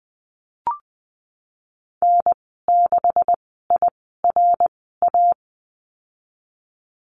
Morse code, heard as a keyed tone, about 20 wpm, spaced as for 15 wpm, 700 Hz. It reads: N6IRA
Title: 179 Lesson 29g - Slash - Call Signs 20wpm